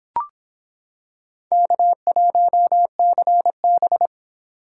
Morse code, heard as a keyed tone, about 25 wpm, 700 Hz. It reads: K1CB